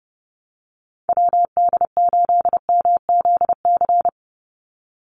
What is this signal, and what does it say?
Morse code, heard as a keyed tone, about 30 wpm, 700 Hz. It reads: WD8MZC
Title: USA Call Signs 30wpm